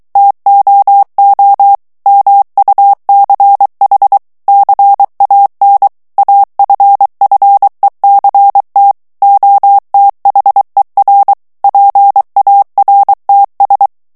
Morse code, heard as a keyed tone, about 23 wpm, 800 Hz. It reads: TOO MUCH CAN AFFECT OTHER PARTS
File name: morse mp3 / ANGLAIS texte clair 1 vitesse 22 mots minute